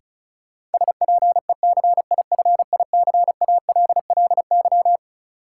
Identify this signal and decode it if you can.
Morse code, heard as a keyed tone, about 34 wpm, 700 Hz. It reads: SPECIFICALLY
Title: Sentences from Top 1100 Words - New Word Drill - 35wpm